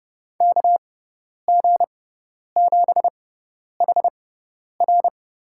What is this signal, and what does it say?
Morse code, heard as a keyed tone, about 30 wpm, spaced as for 10 wpm, 700 Hz. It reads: KG7HR
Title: USA Call Signs - 6x Character Spacing - 30wpm